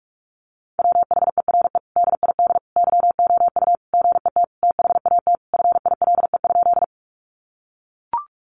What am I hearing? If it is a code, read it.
Morse code, heard as a keyed tone, about 44 wpm, 700 Hz. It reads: WHERE DID YOU GET THAT FILE?
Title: Sentences from Top 1200 Words 45wpm